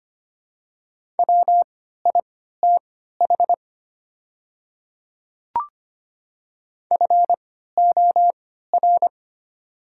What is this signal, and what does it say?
Morse code, heard as a keyed tone, about 25 wpm, spaced as for 14 wpm, 700 Hz. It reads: WITH FOR
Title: 3x Character Spacing - Top 100 Words - 25wpm